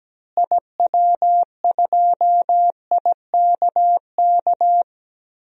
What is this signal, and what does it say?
Morse code, heard as a keyed tone, about 17 wpm, 700 Hz. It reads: IW2IKK
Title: DXCC-100 Call Signs - 6 Characters - Repeated 2x - 17wpm